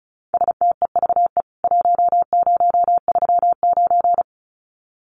Morse code, heard as a keyed tone, about 34 wpm, 700 Hz. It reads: STEVE 1039
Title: CWT Exchange 35wpm